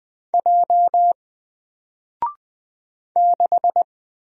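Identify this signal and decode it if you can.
Morse code, heard as a keyed tone, about 20 wpm, 700 Hz. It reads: J 6
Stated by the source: Single letter and number Morse drill at 20 wpm, from N4LCD.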